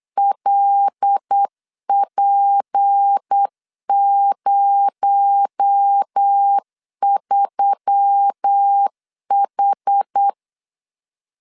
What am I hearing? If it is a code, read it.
Morse code, heard as a keyed tone, about 8 wpm, 800 Hz. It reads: LP03H
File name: LEZIONI / 7 GF9Z / 7 GF9Z 8wpm 3213